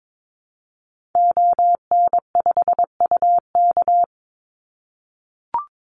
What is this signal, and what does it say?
Morse code, heard as a keyed tone, about 22 wpm, 700 Hz. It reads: ON5UK
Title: Call Signs - 22wpm